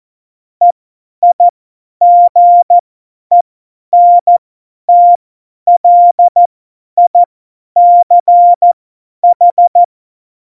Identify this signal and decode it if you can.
Morse code, heard as a keyed tone, about 14 wpm, spaced as for 10 wpm, 700 Hz. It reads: EIGENTLICH
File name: Morsekurs / Witze / Witz 16. April 2019, 50 F70